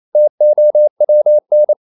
Morse code, heard as a keyed tone, about 28 wpm, 600 Hz. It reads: TOWN
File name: CW / Practice / NationalCapitals28